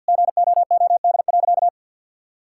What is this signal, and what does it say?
Morse code, heard as a keyed tone, about 50 wpm, 700 Hz. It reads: GOOD.
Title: Sentences from Top 1000 Words - Review Part 5 - 50wpm